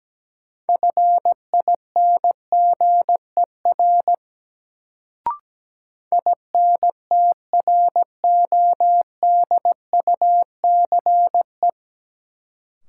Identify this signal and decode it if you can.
Morse code, heard as a keyed tone, about 17 wpm, 700 Hz. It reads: FINGER INTRODUCE
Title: Sentences from Top 1600 Words - New Word Drill Alt - 17wpm